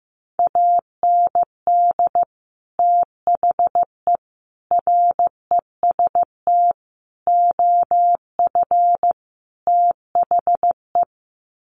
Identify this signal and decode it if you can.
Morse code, heard as a keyed tone, about 15 wpm, 700 Hz. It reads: AND THE REST OF THE
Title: Sets of 5 Words 15wpm